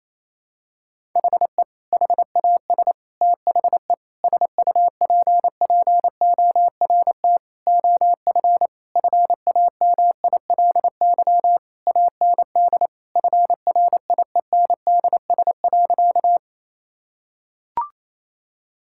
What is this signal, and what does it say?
Morse code, heard as a keyed tone, about 28 wpm, 700 Hz. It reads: HE HAS THE SUPPORT OF FAMILY AND FRIENDS.
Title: Sentences from Top 1000 Words - Review Part 5 - 28wpm